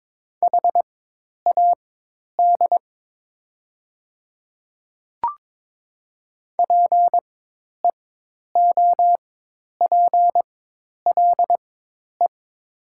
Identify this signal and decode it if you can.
Morse code, heard as a keyed tone, about 22 wpm, spaced as for 10 wpm, 700 Hz. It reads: HAD PEOPLE
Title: 4x Character Spacing - Top 100 Words - 22wpm